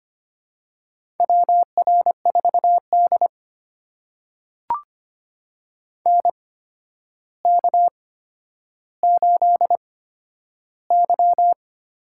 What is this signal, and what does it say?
Morse code, heard as a keyed tone, about 25 wpm, spaced as for 11 wpm, 700 Hz. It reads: WR4D NK8Y
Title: USA Call Signs - 4 Characters - 8x Character Spacing - 25wpm